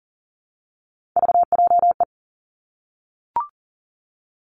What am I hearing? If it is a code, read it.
Morse code, heard as a keyed tone, about 38 wpm, 700 Hz. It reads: VJE
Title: Three Challenging Letter-Number Combinations 40wpm